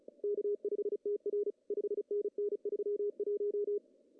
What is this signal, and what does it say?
Morse code, heard as a keyed tone, about 36 wpm, 410 Hz. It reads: K5TR 5NN 31
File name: CW / cqww-cw-k5tr